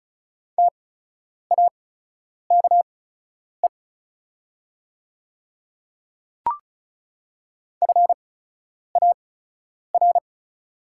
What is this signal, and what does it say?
Morse code, heard as a keyed tone, about 34 wpm, spaced as for 10 wpm, 690 Hz. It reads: TAKE FAR
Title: 8x Character Spacing - Top 300 Words - 35wpm